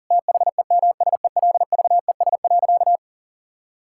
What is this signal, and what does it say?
Morse code, heard as a keyed tone, about 40 wpm, 700 Hz. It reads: THEMSELVES.
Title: Sentences from Top 1000 Words - Review Part 5 - 40wpm